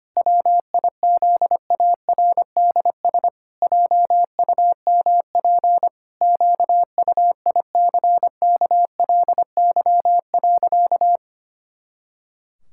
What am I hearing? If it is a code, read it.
Morse code, heard as a keyed tone, about 25 wpm, 700 Hz. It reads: WIZARDS JUMP QUICKLY.
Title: Pangram Challenge v2 - 25wpm